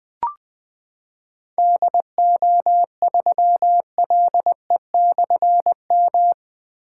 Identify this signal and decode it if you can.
Morse code, heard as a keyed tone, about 20 wpm, 700 Hz. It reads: DO3LE/M
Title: Difficult Call Signs 20wpm